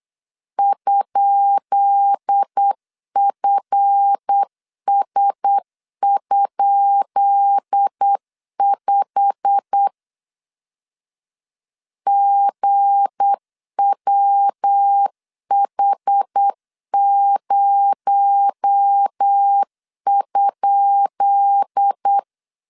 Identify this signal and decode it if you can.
Morse code, heard as a keyed tone, about 8 wpm, 800 Hz. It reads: ?FS?5 GWH0?
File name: LEZIONI / 10 SIGNS / 10B INT 8wpm 3534